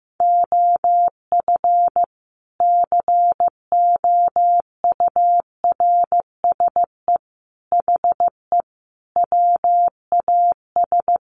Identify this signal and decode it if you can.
Morse code, heard as a keyed tone, about 15 wpm, 700 Hz. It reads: OF COURSE HE WAS
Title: Sentences from Top 500 Words 15wpm